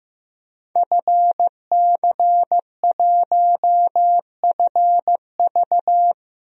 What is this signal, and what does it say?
Morse code, heard as a keyed tone, about 16 wpm, 700 Hz. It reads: FC1FV
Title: DXCC-100 Call Signs - 5 Characters - Repeated 3x - 15wpm